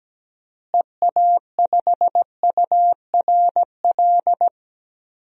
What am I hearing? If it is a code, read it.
Morse code, heard as a keyed tone, about 17 wpm, 700 Hz. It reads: EA5URL
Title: DXCC-100 Call Signs - 6 Characters - Repeated 2x - 17wpm